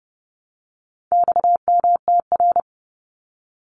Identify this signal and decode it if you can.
Morse code, heard as a keyed tone, about 30 wpm, 700 Hz. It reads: XMTR
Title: Abbreviations Repeated 3x - 30wpm